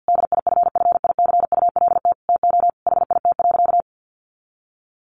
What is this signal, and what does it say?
Morse code, heard as a keyed tone, about 50 wpm, 700 Hz. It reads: DIFFICULT TO HIT.
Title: Sentences from Top 1200 Words 50wpm